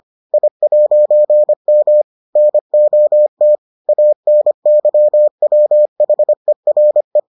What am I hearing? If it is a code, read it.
Morse code, heard as a keyed tone, about 25 wpm, 600 Hz. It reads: I'M NOT ANYWHERE